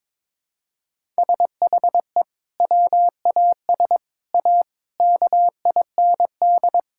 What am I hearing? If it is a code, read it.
Morse code, heard as a keyed tone, about 22 wpm, 700 Hz. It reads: SHE WAS A KIND
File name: practice / 22wpm / Sentences with 3-Word Phrases - 22wpm